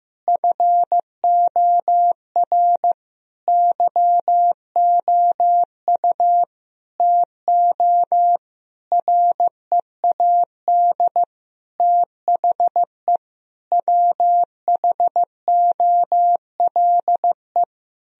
Morse code, heard as a keyed tone, about 16 wpm, 700 Hz. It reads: FOR YOU TO READ THE WHOLE
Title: Sentences from Top 1000 Words - Review Part 1 - 15wpm